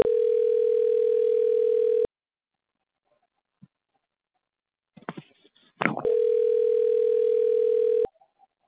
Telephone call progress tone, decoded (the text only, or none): ringback tone